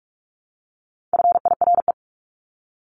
Morse code, heard as a keyed tone, about 44 wpm, 700 Hz. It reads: FIRE
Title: Top 500 Words Repeated 45wpm